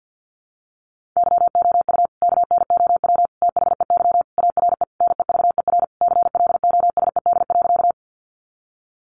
Morse code, heard as a keyed tone, about 50 wpm, 700 Hz. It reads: YOU KNOW THEY ARE NEVER CLOSED.
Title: Sentences from Top 1400 Words 50wpm